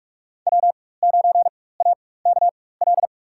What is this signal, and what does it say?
Morse code, heard as a keyed tone, about 43 wpm, spaced as for 21 wpm, 700 Hz. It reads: W9AKL
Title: USA Call Signs - 5 Characters - 4x Character Spacing - 45wpm